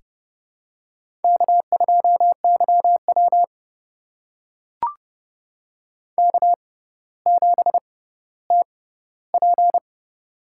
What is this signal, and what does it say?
Morse code, heard as a keyed tone, about 28 wpm, spaced as for 15 wpm, 700 Hz. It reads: K2YW K7TP
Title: USA Call Signs - 4 Characters - 6x Character Spacing - 30wpm